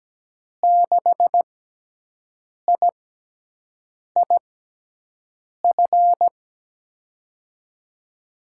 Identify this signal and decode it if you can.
Morse code, heard as a keyed tone, about 17 wpm, spaced as for 6 wpm, 700 Hz. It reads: 6IIF
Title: USA Call Signs - 6 Characters - 6x Character Spacing - 17wpm